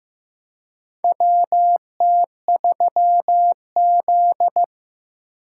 Morse code, heard as a keyed tone, about 15 wpm, 700 Hz. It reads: WT3Z
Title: USA Call Signs - 4 Characters - 4x Character Spacing - 15wpm